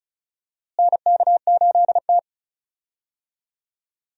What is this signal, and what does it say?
Morse code, heard as a keyed tone, about 36 wpm, 700 Hz. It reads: NK8T
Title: USA Call Signs - 4 Characters - Repeated 3x - 35wpm